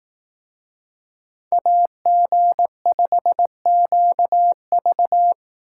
Morse code, heard as a keyed tone, about 16 wpm, 700 Hz. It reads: AG5QV
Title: Speed Racing - USA Call Signs 15wpm